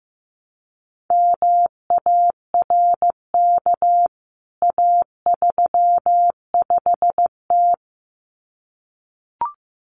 Morse code, heard as a keyed tone, about 15 wpm, 700 Hz. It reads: MARK A35T
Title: CWT Exchange 15wpm